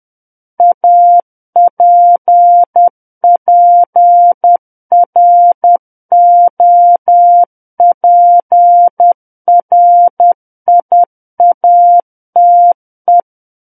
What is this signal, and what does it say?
Morse code, heard as a keyed tone, about 10 wpm, 700 Hz. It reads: APPROPRIATE
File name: Book One / 01-Chapter 1 - 10 WPM